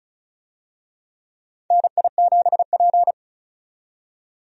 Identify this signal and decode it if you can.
Morse code, heard as a keyed tone, about 34 wpm, 700 Hz. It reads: NI7P
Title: USA Call Signs - 4 Characters - 35wpm